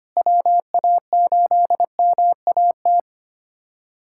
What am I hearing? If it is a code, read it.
Morse code, heard as a keyed tone, about 25 wpm, 700 Hz. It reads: WA8MAT
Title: USA Callsign DE Callsign 25wpm